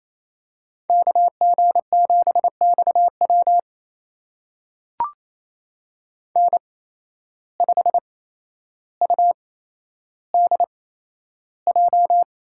Morse code, heard as a keyed tone, about 28 wpm, spaced as for 12 wpm, 700 Hz. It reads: KG7XW N5UDJ
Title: USA Call Signs - 5 Characters - 8x Character Spacing - 28wpm